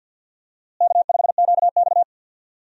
Morse code, heard as a keyed tone, about 51 wpm, 700 Hz. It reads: K5QX
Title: USA Call Signs - 4 Characters - Repeated 3x - 50wpm